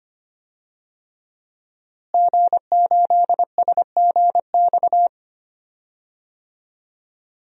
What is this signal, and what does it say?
Morse code, heard as a keyed tone, about 25 wpm, 700 Hz. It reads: G8SGX
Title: DXCC-100 Call Signs - 5 Characters - Repeated 3x - 25wpm